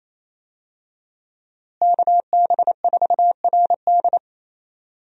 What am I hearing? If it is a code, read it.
Morse code, heard as a keyed tone, about 27 wpm, 700 Hz. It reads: KB4RD
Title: USA Call Signs - 5 Characters - Repeated 2x - 28wpm